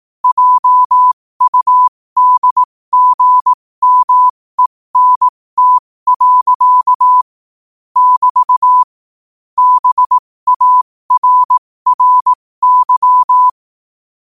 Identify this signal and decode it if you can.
Morse code, heard as a keyed tone, about 18 wpm, spaced as for 15 wpm, 1000 Hz. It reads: JUDGMENT. <BT> BARRY
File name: qotdfiles / 15wpm